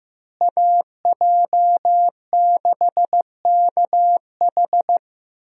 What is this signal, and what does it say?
Morse code, heard as a keyed tone, about 15 wpm, 700 Hz. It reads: AJ6KH
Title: USA Call Signs - 5 Characters - Repeated 3x - 15wpm